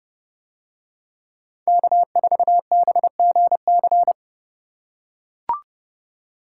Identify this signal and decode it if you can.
Morse code, heard as a keyed tone, about 30 wpm, 700 Hz. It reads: K4BGC